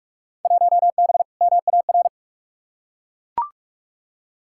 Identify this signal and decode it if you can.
Morse code, heard as a keyed tone, about 41 wpm, 700 Hz. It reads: 1B MAR
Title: Field Day Exchanges Encore - 45wpm